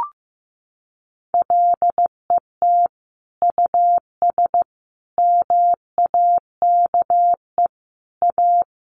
Morse code, heard as a keyed tone, about 15 wpm, 700 Hz. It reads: LET US MAKE A
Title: Sentences from Top 1500 Words 15wpm